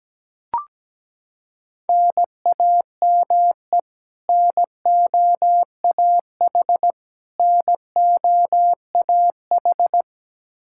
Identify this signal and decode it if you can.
Morse code, heard as a keyed tone, about 17 wpm, 700 Hz. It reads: NAME NOAH NOAH
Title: Practice QSO - Exchange 1 - Beginning - 17wpm